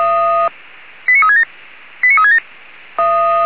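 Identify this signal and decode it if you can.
Five-tone selective call (ZVEI), heard as quick runs of five tones, tones of about 70 ms. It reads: 89367 89367
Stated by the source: Five-tone ZVEI selective call 89366, received by SDR.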